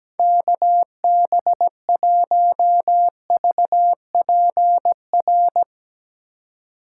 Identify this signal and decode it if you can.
Morse code, heard as a keyed tone, about 17 wpm, 700 Hz. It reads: KB1VPR
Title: Speed Racing - USA 2x3 Call Signs 17wpm